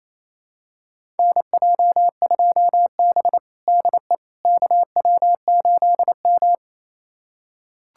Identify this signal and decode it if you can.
Morse code, heard as a keyed tone, about 28 wpm, 700 Hz. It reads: NJ2B DE KW8M
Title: USA Callsign DE Callsign - 4 Characters - 28wpm